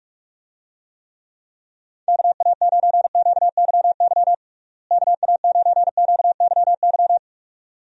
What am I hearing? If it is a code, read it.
Morse code, heard as a keyed tone, about 44 wpm, 700 Hz. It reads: KA9QYY KA9QYY
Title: USA Call Signs - 6 Characters - Repeated 2x - 45wpm